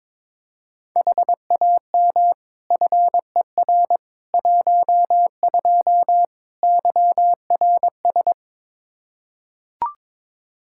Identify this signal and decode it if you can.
Morse code, heard as a keyed tone, about 22 wpm, 700 Hz. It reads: HAM FER 12 YRS
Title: Practice QSO - Exchange 3 - More Info - 22wpm